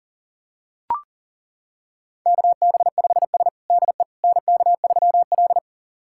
Morse code, heard as a keyed tone, about 40 wpm, 700 Hz. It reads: KB5S DE NK3L
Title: USA Callsign DE Callsign - 4 Characters - 40wpm